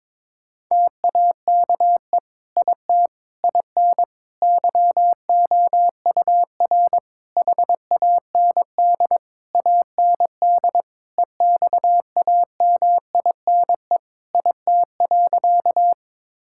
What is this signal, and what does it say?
Morse code, heard as a keyed tone, about 22 wpm, 700 Hz. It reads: TAKE IT IN YOUR HAND AND EXAMINE IT.
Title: Sentences from Top 1600 Words 22wpm